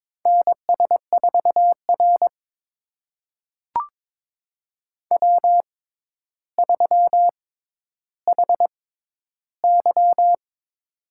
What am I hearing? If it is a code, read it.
Morse code, heard as a keyed tone, about 21 wpm, spaced as for 11 wpm, 700 Hz. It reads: NS4R W3HY
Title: USA Call Signs - 4 Characters - 6x Character Spacing - 22wpm